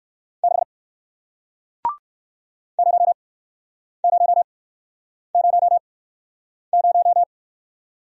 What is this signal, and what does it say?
Morse code, heard as a keyed tone, about 55 wpm, spaced as for 9 wpm, 700 Hz. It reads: 6 0000
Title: Speed-Racing Single Letter-Number 45wpm